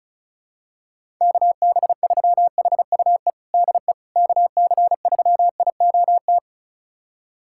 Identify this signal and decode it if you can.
Morse code, heard as a keyed tone, about 36 wpm, 700 Hz. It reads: KB3HUE DE KC3IOT
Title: USA Callsign DE Callsign - 6 Characters - 35wpm